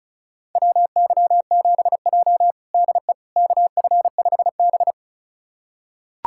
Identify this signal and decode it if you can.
Morse code, heard as a keyed tone, about 36 wpm, 700 Hz. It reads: WY7J DE KF5B
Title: USA Callsign DE Callsign - 4 Characters - 35wpm